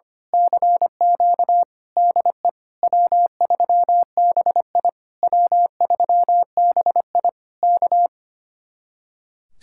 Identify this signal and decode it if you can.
Morse code, heard as a keyed tone, about 24 wpm, 700 Hz. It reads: CQ DE W3BI W3BI K
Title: Speed-Racing Common QSO Phrases Encore - 25wpm